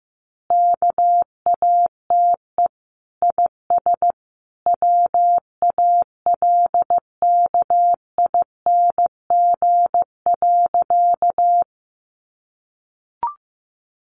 Morse code, heard as a keyed tone, about 15 wpm, 700 Hz. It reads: KATE IS WALKING.
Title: Easy Phrases 15wpm